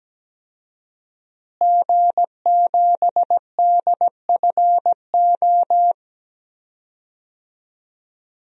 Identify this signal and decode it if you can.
Morse code, heard as a keyed tone, about 17 wpm, 700 Hz. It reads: G7DFO
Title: DXCC-100 Call Signs - 5 Characters - Repeated 2x - 17wpm